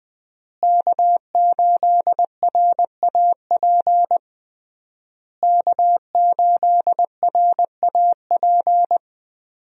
Morse code, heard as a keyed tone, about 21 wpm, 700 Hz. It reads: K8RAP K8RAP
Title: USA Call Signs - 5 Characters - Repeated 3x - 20wpm